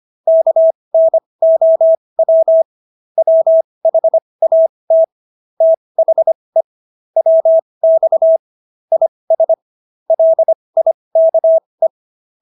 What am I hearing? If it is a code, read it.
Morse code, heard as a keyed tone, about 25 wpm, spaced as for 20 wpm, 650 Hz. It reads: KNOW WHAT THE WX IS LIKE